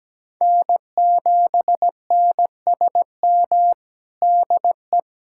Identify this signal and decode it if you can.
Morse code, heard as a keyed tone, about 17 wpm, 700 Hz. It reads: N7NSM DE
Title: USA Callsign DE Callsign - 5 Characters - 17wpm